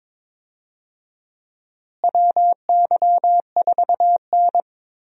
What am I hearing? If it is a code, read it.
Morse code, heard as a keyed tone, about 22 wpm, 700 Hz. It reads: WY4N
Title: USA Call Signs - 4 Characters - Repeated 2x - 22wpm